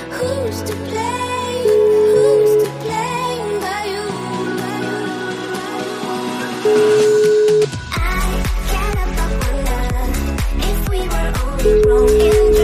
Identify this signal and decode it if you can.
Ringback tone (Europe, 425 Hz), with noise.